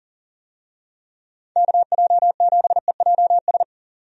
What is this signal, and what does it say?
Morse code, heard as a keyed tone, about 40 wpm, 700 Hz. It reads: KJ7EJS
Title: USA Call Signs - 6 Characters - 40wpm